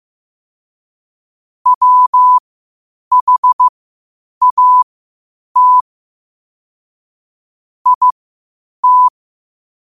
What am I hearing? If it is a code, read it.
Morse code, heard as a keyed tone, about 15 wpm, spaced as for 8 wpm, 1000 Hz. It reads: WHAT IT
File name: qotdfiles / 8wpm